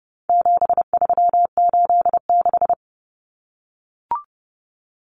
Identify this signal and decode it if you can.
Morse code, heard as a keyed tone, about 30 wpm, 700 Hz. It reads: 7386
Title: Common Cut Numbers - 4-Digit 30wpm